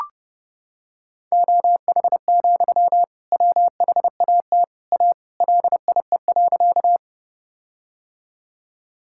Morse code, heard as a keyed tone, about 30 wpm, 700 Hz. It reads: OH, WHAT A LIE.
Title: Sentences from Top 1600 Words 30wpm